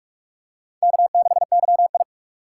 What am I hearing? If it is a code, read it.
Morse code, heard as a keyed tone, about 44 wpm, 700 Hz. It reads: K6YI
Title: USA Call Signs - 4 Characters - 6x Character Spacing - 45wpm